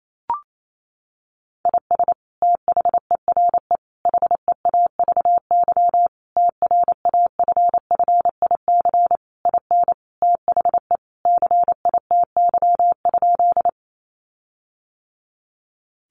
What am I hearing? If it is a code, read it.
Morse code, heard as a keyed tone, about 28 wpm, 700 Hz. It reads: IS THERE HEAVY TRAFFIC IN THE CITY?
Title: Sentences from Top 1300 Words 28wpm